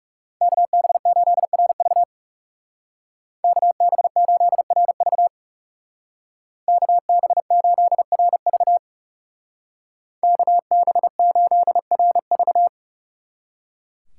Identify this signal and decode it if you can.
Morse code, heard as a keyed tone, about 39 wpm, 700 Hz. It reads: KB8RV KB8RV KB8RV KB8RV